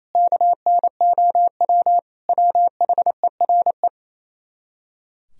Morse code, heard as a keyed tone, about 37 wpm, 700 Hz. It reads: KNOW WHERE